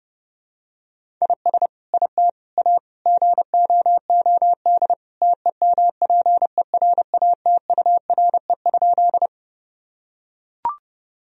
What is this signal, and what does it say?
Morse code, heard as a keyed tone, about 30 wpm, 700 Hz. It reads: IS IT A GOOD TEMPERATURE?
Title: Sentences from Top 1000 Words - Review Part 4 - 30wpm